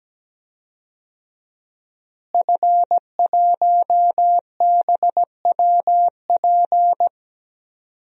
Morse code, heard as a keyed tone, about 17 wpm, 700 Hz. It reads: F1BWP
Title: DXCC-100 Call Signs - 5 Characters - Repeated 2x - 17wpm